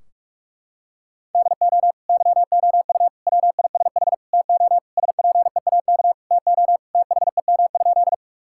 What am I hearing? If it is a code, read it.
Morse code, heard as a keyed tone, about 46 wpm, 700 Hz. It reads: DO YOU WISH TO SPEAK TO THEM?
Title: Sentences from Top 1600 Words 45wpm